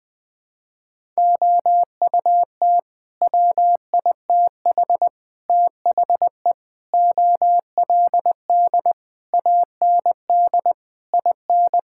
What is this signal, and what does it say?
Morse code, heard as a keyed tone, about 20 wpm, 700 Hz. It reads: OUT WITH THE OLD AND IN
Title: Sets of 9 Words - 20wpm